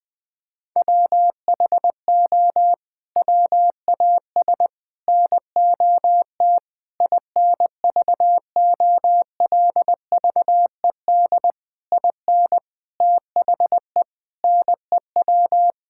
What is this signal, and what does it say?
Morse code, heard as a keyed tone, about 21 wpm, 700 Hz. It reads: WHO WAS NOT INVOLVED IN THE NEW
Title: Sets of 8 Words - 20wpm